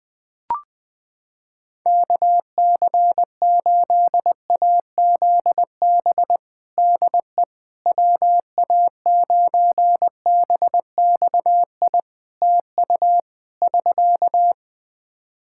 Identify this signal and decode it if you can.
Morse code, heard as a keyed tone, about 20 wpm, 700 Hz. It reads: KC8AZB DE WA9BXI TU <SK>
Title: K1USN SST QSO Practice 20wpm